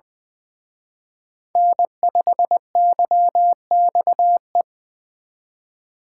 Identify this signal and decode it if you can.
Morse code, heard as a keyed tone, about 24 wpm, 700 Hz. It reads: N5YXE
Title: Speed Racing - USA 1x3 Call Signs 20wpm